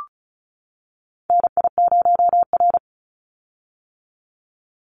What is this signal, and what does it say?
Morse code, heard as a keyed tone, about 34 wpm, 700 Hz. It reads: NI0R
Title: Call Signs - 35wpm